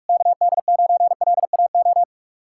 Morse code, heard as a keyed tone, about 60 wpm, spaced as for 52 wpm, 700 Hz. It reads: KD9LAO
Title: Speed-Racing Common QSO Phrases - 45wpm